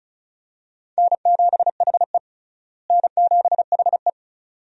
Morse code, heard as a keyed tone, about 36 wpm, 700 Hz. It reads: N7HE N7HE
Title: USA Call Signs - 4 Characters - Repeated 2x - 35wpm